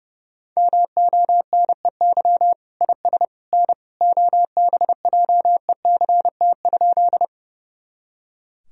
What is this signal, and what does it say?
Morse code, heard as a keyed tone, about 32 wpm, 700 Hz. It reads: MONEY IS N OBJECT?
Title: Sentences from Top 500 Words and Abbreviations 30wpm